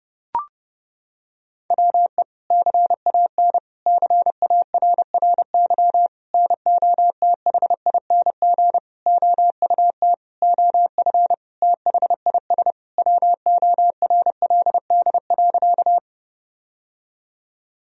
Morse code, heard as a keyed tone, about 31 wpm, 700 Hz. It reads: WE CAN CARRY NOTHING OUT OF THIS WORLD.
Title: Sentences from Top 1000 Words - Review Part 1 - 30wpm